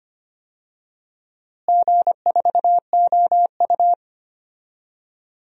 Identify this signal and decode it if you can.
Morse code, heard as a keyed tone, about 25 wpm, 700 Hz. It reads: G4OU
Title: DXCC-100 Call Signs - 4 Characters - Repeated 3x - 25wpm